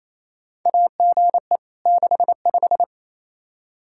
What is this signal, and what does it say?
Morse code, heard as a keyed tone, about 28 wpm, 700 Hz. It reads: AGE 65
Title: K1USN SST QSO Practice 28wpm